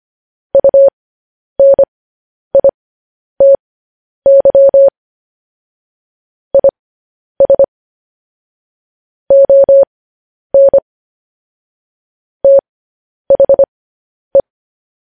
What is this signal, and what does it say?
Morse code, heard as a keyed tone, about 25 wpm, spaced as for 10 wpm, 550 Hz. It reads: UNITY IS ON THE
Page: news headlines in morse code at 10 wpm